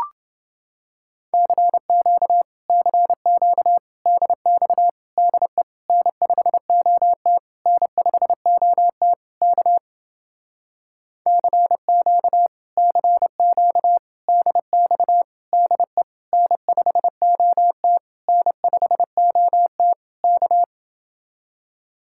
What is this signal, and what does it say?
Morse code, heard as a keyed tone, about 30 wpm, 700 Hz. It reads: CQ CQ DX DE N5OT N5OT K CQ CQ DX DE N5OT N5OT K